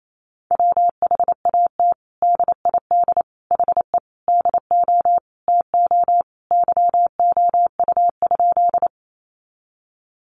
Morse code, heard as a keyed tone, about 28 wpm, 700 Hz. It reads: WHAT DID HE DO TO YOU?
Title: Sentences from Top 100 Words 28wpm